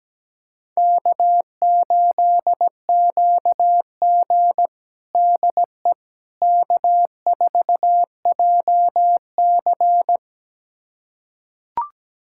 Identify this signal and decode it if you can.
Morse code, heard as a keyed tone, about 17 wpm, 700 Hz. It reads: K8QG DE K4JC
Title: USA Callsign DE Callsign - 4 Characters - 17wpm